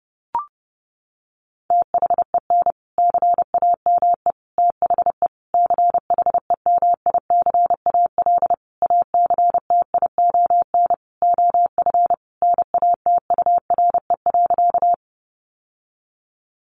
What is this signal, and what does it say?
Morse code, heard as a keyed tone, about 28 wpm, 700 Hz. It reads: THEN CAME THE CHEMICAL ACTION OF NATURE.
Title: Sentences from Top 1200 Words 30wpm